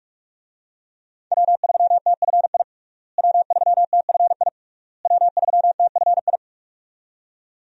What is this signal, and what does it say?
Morse code, heard as a keyed tone, about 41 wpm, 700 Hz. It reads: W3TFI W3TFI W3TFI
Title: USA Call Signs - 5 Characters - Repeated 3x - 45wpm